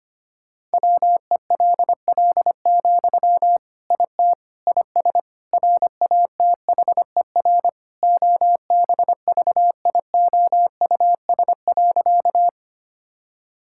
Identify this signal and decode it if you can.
Morse code, heard as a keyed tone, about 25 wpm, 700 Hz. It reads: WELL, IT IS RATHER OBVIOUS.
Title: Sentences from Top 1600 Words 25wpm